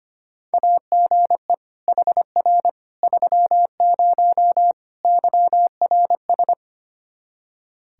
Morse code, heard as a keyed tone, about 25 wpm, 700 Hz. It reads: AGE HR 30 YRS
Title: Practice QSO - Exchange 3 - More Info - 25wpm